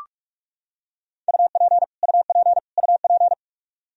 Morse code, heard as a keyed tone, about 44 wpm, 700 Hz. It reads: UP UP UP
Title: Top 100 Words Repeated 45wpm